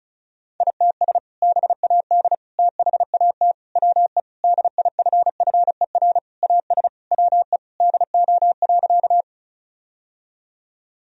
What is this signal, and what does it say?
Morse code, heard as a keyed tone, about 34 wpm, 690 Hz. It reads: ITS BAD THAT WE DIFFER AS WE DO.
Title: Sentences from Top 1000 Words - Review Part 1 - 35wpm